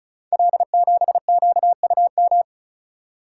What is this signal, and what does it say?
Morse code, heard as a keyed tone, about 36 wpm, 700 Hz. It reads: L7QUM
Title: Code Groups - 5 Letter:Numbers - 35wpm